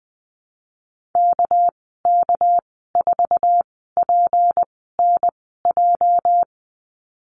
Morse code, heard as a keyed tone, about 19 wpm, spaced as for 14 wpm, 700 Hz. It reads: KK4PNJ
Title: 219 Lesson 35g - Number 0 - Call Signs 20wpm